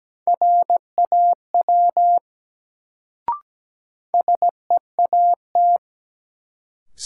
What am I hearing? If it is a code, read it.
Morse code, heard as a keyed tone, about 17 wpm, 700 Hz. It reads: RAW SEAT
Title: Sentences from Top 1500 Words - New Word Drill Alt - 17wpm